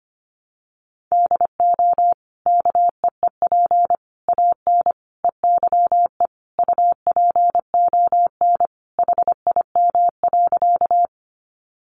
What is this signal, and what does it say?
Morse code, heard as a keyed tone, about 25 wpm, 700 Hz. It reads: DO KEEP AN EYE UPON HIM.
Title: Sentences from Top 1100 Words 25wpm